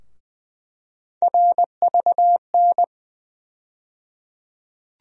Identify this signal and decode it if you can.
Morse code, heard as a keyed tone, about 21 wpm, 700 Hz. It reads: RVN
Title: USA Call Sign Postfixes - 3 Letters 20wpm